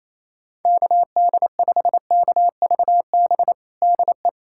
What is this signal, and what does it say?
Morse code, heard as a keyed tone, about 27 wpm, 700 Hz. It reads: KD5KVB DE
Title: USA Callsign DE Callsign - 6 Characters - 28wpm